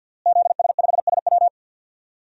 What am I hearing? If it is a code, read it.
Morse code, heard as a keyed tone, about 48 wpm, 700 Hz. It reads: ZS5SW
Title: DXCC-100 Call Signs - 5 Characters - Repeated 2x - 50wpm